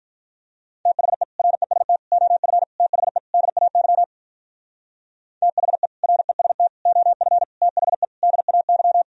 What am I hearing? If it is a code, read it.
Morse code, heard as a keyed tone, about 43 wpm, 700 Hz. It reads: THE REST OF THE DAY THE REST OF THE DAY
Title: Speed-Racing - Sets of 5 Words 35wpm